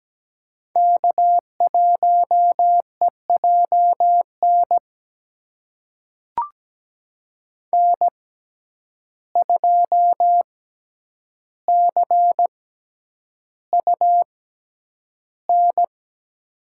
Morse code, heard as a keyed tone, about 17 wpm, spaced as for 9 wpm, 700 Hz. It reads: K1EJN N2CUN
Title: USA Call Signs - 5 Characters - 6x Character Spacing - 17wpm